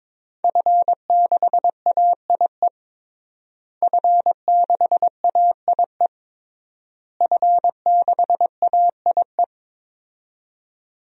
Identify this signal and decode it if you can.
Morse code, heard as a keyed tone, about 22 wpm, 700 Hz. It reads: F6AIE F6AIE F6AIE
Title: DXCC-100 Call Signs - 5 Characters - Repeated 3x - 22wpm